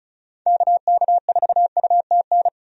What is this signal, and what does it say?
Morse code, heard as a keyed tone, about 34 wpm, 700 Hz. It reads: KK4UTN
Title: USA Call Signs - 6 Characters - 35wpm